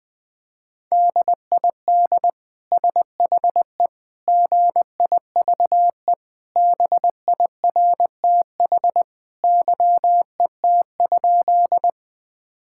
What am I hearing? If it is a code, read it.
Morse code, heard as a keyed tone, about 20 wpm, 700 Hz. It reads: DID SHE GIVE BIRTH YET?
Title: Sentences from Top 1000 Words - Review Part 5 - 20wpm